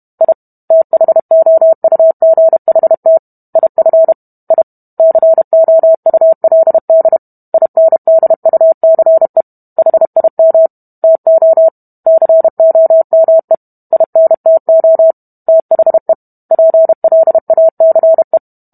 Morse code, heard as a keyed tone, about 32 wpm, 650 Hz. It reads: I THOUGHT IF I COULD INDUCE HIM TO COME INTO THE PLACE